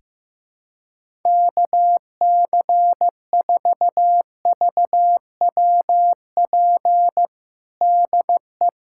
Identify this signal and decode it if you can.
Morse code, heard as a keyed tone, about 15 wpm, 700 Hz. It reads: KC4VWP DE
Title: USA Callsign DE Callsign 15wpm